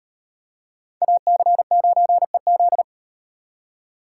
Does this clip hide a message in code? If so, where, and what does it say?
Yes: Morse code from 1.0 to 2.8 s, heard as a keyed tone, about 34 wpm, 700 Hz. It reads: AC9EZ